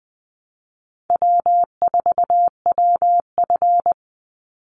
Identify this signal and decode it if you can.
Morse code, heard as a keyed tone, about 20 wpm, 700 Hz. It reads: W4WF
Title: Easy Call Signs Repeated 3x - 20wpm